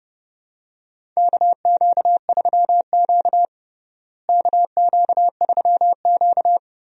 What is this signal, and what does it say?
Morse code, heard as a keyed tone, about 32 wpm, 700 Hz. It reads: KQ3Q KQ3Q
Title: USA Call Signs - 4 Characters - Repeated 3x - 30wpm